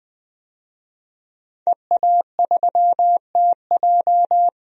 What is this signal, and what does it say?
Morse code, heard as a keyed tone, about 21 wpm, 700 Hz. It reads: EA3TJ
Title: DXCC-100 Call Signs - 5 Characters - Repeated 3x - 20wpm